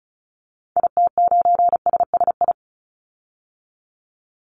Morse code, heard as a keyed tone, about 34 wpm, 700 Hz. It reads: IT9SSI
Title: Call Signs Repeated 2x - 35wpm